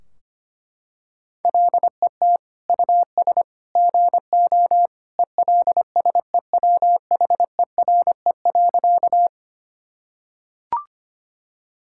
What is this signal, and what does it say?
Morse code, heard as a keyed tone, about 25 wpm, 700 Hz. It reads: LET US GO ELSEWHERE.
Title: Sentences from Top 1600 Words 25wpm